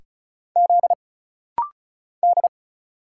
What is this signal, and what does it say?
Morse code, heard as a keyed tone, about 35 wpm, 700 Hz. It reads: Z D
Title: Single Letters - Rapid-Fire 35wpm